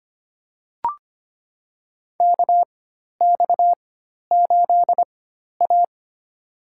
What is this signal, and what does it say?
Morse code, heard as a keyed tone, about 25 wpm, spaced as for 12 wpm, 700 Hz. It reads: KX8A
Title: USA Call Signs - 4 Characters - 4x Character Spacing - 25wpm